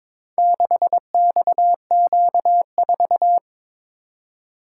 Morse code, heard as a keyed tone, about 22 wpm, 700 Hz. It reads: 6XQ4
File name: practice / 22wpm / Four Letter-Number Combinations - 22wpm